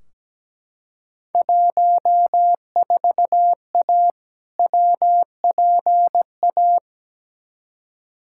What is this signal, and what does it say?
Morse code, heard as a keyed tone, about 17 wpm, 700 Hz. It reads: 14A WPA
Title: Field Day Exchanges Encore - 17wpm